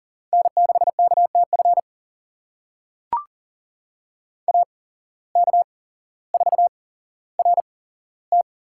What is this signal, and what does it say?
Morse code, heard as a keyed tone, about 38 wpm, spaced as for 17 wpm, 700 Hz. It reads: N6KTF AK4RT
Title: USA Call Signs - 5 Characters - 8x Character Spacing - 40wpm